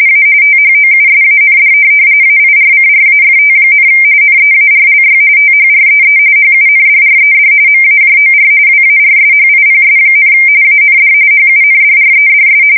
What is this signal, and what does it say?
RTTY, about 45.45 baud. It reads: A THOUSAND AIRCRAFT, NEARLY THREE HUNDRED MAJOR WARSHIPS, AND HUNDREDS
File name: fsk170 / USS-New-Jersey